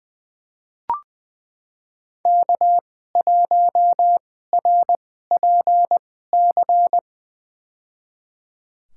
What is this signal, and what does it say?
Morse code, heard as a keyed tone, about 20 wpm, spaced as for 14 wpm, 700 Hz. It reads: K1RPC
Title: USA Call Signs - 2x Character Spacing - 20wpm